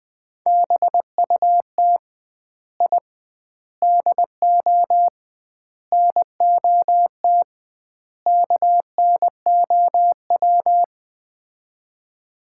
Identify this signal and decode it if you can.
Morse code, heard as a keyed tone, about 21 wpm, 700 Hz. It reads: BUT I DO NOT KNOW
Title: 2x Word-Spacing - Sets of 5 Words - 20wpm